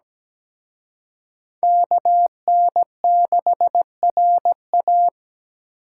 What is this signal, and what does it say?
Morse code, heard as a keyed tone, about 20 wpm, 700 Hz. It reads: KN6RA
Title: Speed Racing - USA 2x2 Call Signs 17wpm